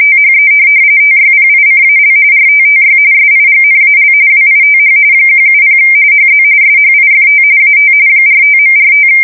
RTTY, about 45.45 baud. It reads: THE TYPICAL RTTY SIGNAL WITH 170 HZ SHIFT AT 45.45
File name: fsk170 / teletype-RTTY